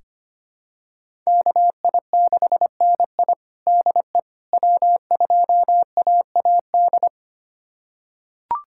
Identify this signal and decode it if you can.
Morse code, heard as a keyed tone, about 25 wpm, 700 Hz. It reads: KI6NI DE W2AAD
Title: USA Callsign DE Callsign - 5 Characters - 25wpm